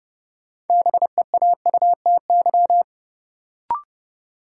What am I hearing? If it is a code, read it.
Morse code, heard as a keyed tone, about 31 wpm, 700 Hz. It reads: BEAUTY
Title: Sentences from Top 600 Words - New Word Drill - 30wpm